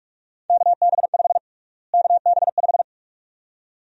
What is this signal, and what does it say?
Morse code, heard as a keyed tone, about 48 wpm, 700 Hz. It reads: KB5 KB5
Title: USA Call Sign Prefixes - Repeated 2x - 45wpm